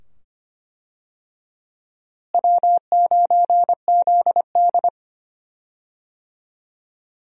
Morse code, heard as a keyed tone, about 25 wpm, 700 Hz. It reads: W9ZD
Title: USA Call Signs - 4 Characters - Repeated 2x - 25wpm